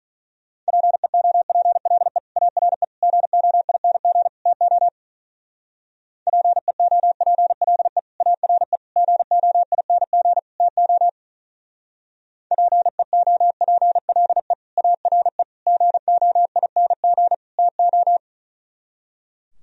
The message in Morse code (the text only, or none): PEOPLE ARE GOING TO PEOPLE ARE GOING TO PEOPLE ARE GOING TO